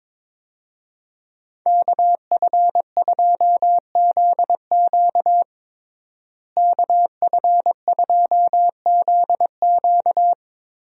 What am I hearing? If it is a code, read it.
Morse code, heard as a keyed tone, about 22 wpm, 700 Hz. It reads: KF2ZQ KF2ZQ
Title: USA Call Signs - 5 Characters - Repeated 3x - 22wpm